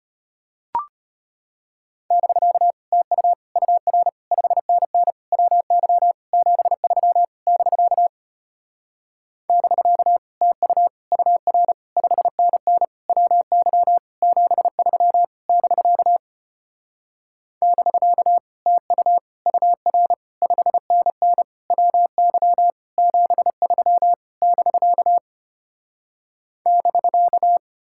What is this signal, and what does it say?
Morse code, heard as a keyed tone, about 33 wpm, 700 Hz. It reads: <BK> TU UR 5NN WY 73 <BK> <BK> TU UR 5NN WY 73 <BK> <BK> TU UR 5NN WY 73 <BK> <BK>